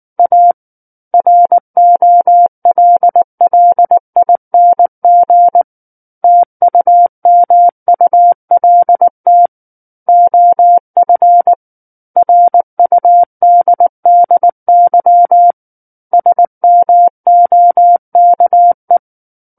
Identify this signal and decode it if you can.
Morse code, heard as a keyed tone, about 19 wpm, 700 Hz. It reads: A ROLLING TUMULT OF RUDDY SMOKE
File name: Book One / 10-Chapter 10 - 19 WPM